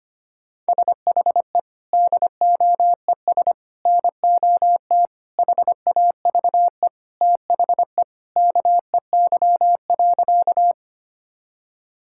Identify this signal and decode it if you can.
Morse code, heard as a keyed tone, about 25 wpm, 700 Hz. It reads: SHE DOES NOT HAVE THE KEY.
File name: practice / 25wpm / Sentences with 3-Word Phrases - 25wpm